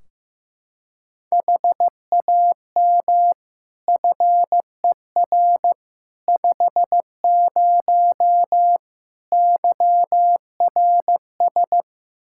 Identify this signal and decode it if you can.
Morse code, heard as a keyed tone, about 15 wpm, 700 Hz. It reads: HAM FER 50 YRS
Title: Practice QSO - Exchange 3 - More Info - 15wpm